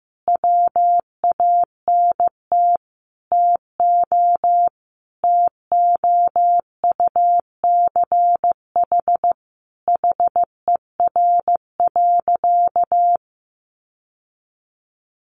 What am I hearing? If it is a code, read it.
Morse code, heard as a keyed tone, about 15 wpm, 700 Hz. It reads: WANT TO TOUCH HER.
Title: Sentences from Top 1200 Words 15wpm